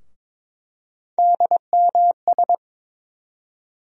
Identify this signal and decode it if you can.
Morse code, heard as a keyed tone, about 22 wpm, 700 Hz. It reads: DMS
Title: USA Call Sign Postfixes - 3 Letters 22wpm